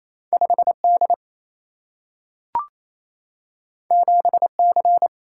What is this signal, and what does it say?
Morse code, heard as a keyed tone, about 28 wpm, 700 Hz. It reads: 5 D 7 C
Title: Single-Letter Single-Number 28wpm